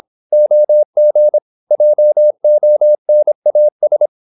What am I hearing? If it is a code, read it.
Morse code, heard as a keyed tone, about 26 wpm, 600 Hz. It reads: OG JONAS
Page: Ferie 26 wpm